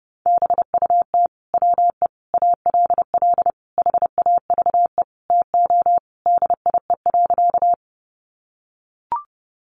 Morse code, heard as a keyed tone, about 30 wpm, 700 Hz. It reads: BUT WE ALL HAVE TO DIE.
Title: Sentences from Top 1300 Words 30wpm